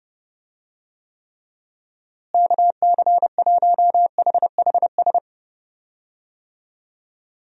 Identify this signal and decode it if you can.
Morse code, heard as a keyed tone, about 32 wpm, 700 Hz. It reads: KC1HHS
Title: USA Call Signs - 6 Characters - Repeated 3x - 30wpm